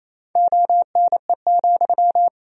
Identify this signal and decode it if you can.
Morse code, heard as a keyed tone, about 28 wpm, 700 Hz. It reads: ONE,
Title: Sentences from Top 1000 Words - Review Part 1 - 28wpm